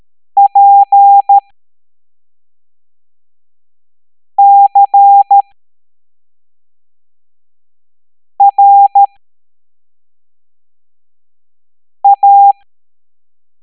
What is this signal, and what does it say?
Morse code, heard as a keyed tone, about 13 wpm, spaced as for 3 wpm, 800 Hz. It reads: PCRA